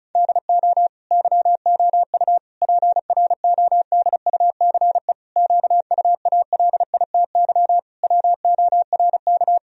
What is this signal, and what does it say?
Morse code, heard as a keyed tone, about 36 wpm, 700 Hz. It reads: DO YOU PRODUCE QUALITY WORK
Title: Sentences from Top 1000 Words - Review Part 4 - 35wpm